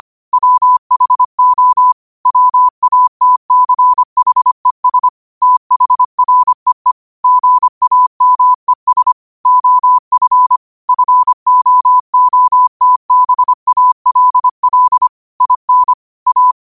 Morse code, heard as a keyed tone, about 25 wpm, 1000 Hz. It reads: WHO WATCHES THREE GAMES OF FOOTBALL IN A
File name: qotdfiles / 25wpm